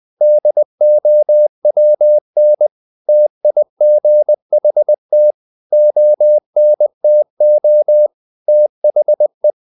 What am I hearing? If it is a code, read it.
Morse code, heard as a keyed tone, about 20 wpm, 600 Hz. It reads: DOWN TIGHT ONTO THE
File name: the old man and the sea part 2 20wpm